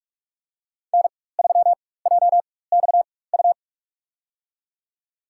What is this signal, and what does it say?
Morse code, heard as a keyed tone, about 46 wpm, spaced as for 21 wpm, 700 Hz. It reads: N3JXU
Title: USA Call Signs - 5 Characters - 4x Character Spacing - 45wpm